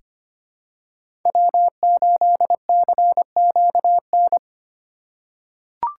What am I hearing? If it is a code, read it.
Morse code, heard as a keyed tone, about 25 wpm, 700 Hz. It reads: W8CQN
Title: USA Call Signs - 5 Characters - 3x Character Spacing - 25wpm